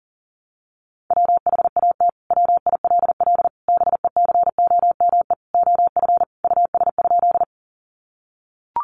Morse code, heard as a keyed tone, about 37 wpm, 700 Hz. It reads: WHAT WILL BECOME OF US?
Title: Sentences from Top 600 Words 40wpm